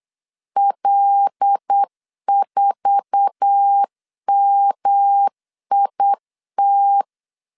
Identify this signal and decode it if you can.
Morse code, heard as a keyed tone, about 8 wpm, 800 Hz. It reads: L4MIT
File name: LEZIONI / 10 SIGNS / 10B INT 8wpm 3534